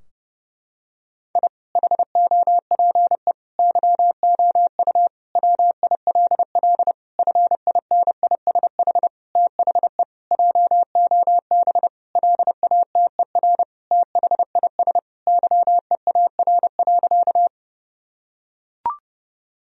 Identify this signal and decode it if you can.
Morse code, heard as a keyed tone, about 30 wpm, 700 Hz. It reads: I HOPE YOU WILL FINISH THE JOB LATER THIS YEAR.
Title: Sentences from Top 1000 Words - Review Part 3 - 30wpm